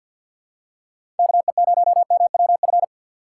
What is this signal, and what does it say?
Morse code, heard as a keyed tone, about 50 wpm, 700 Hz. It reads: KE0MWF
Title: USA Call Signs - 3x Character Spacing - 50wpm